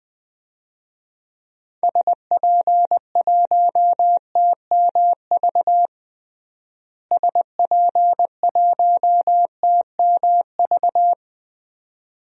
Morse code, heard as a keyed tone, about 20 wpm, 700 Hz. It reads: SP1TMV SP1TMV
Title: DXCC-100 Call Signs - 6 Characters - Repeated 2x - 20wpm